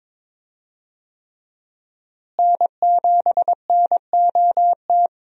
Morse code, heard as a keyed tone, about 23 wpm, 700 Hz. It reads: N7NOT
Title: USA Call Signs - 5 Characters - Repeated 3x - 22wpm